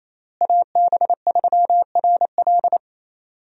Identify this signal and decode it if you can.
Morse code, heard as a keyed tone, about 28 wpm, 700 Hz. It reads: AB3RL
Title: USA Call Signs - 5 Characters - Repeated 3x - 28wpm